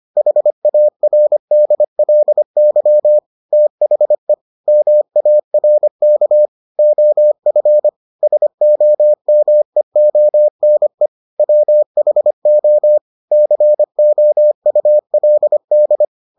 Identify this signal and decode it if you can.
Morse code, heard as a keyed tone, about 25 wpm, 600 Hz. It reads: HARDLY THE MARK OF SOMEONE WHO COULD